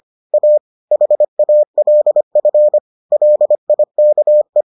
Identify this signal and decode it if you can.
Morse code, heard as a keyed tone, about 25 wpm, 600 Hz. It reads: A HALF LIKE